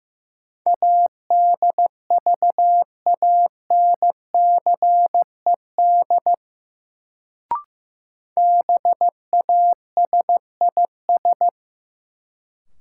Morse code, heard as a keyed tone, about 15 wpm, 700 Hz. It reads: ADVANCED BASIS
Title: Sentences from Top 1200 Words - New Word Drill Alt - 15wpm